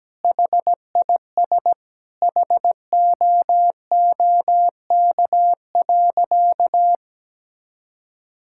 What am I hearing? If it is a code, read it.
Morse code, heard as a keyed tone, about 17 wpm, 700 Hz. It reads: HIS HOOK.
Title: Sentences from Top 1600 Words 17wpm